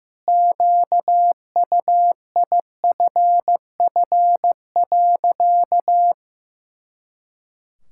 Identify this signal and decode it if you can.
Morse code, heard as a keyed tone, about 15 wpm, 700 Hz. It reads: QUIFF.
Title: Pangram Challenge v2 - 15wpm